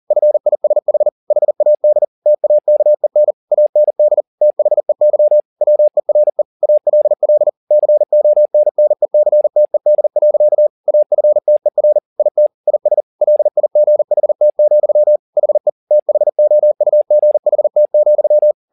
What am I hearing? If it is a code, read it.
Morse code, heard as a keyed tone, about 40 wpm, 600 Hz. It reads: FISH HAD TAKEN AND THEY WERE ALL CONNECTED. AFTER IT IS LIGHT, HE THOUGHT,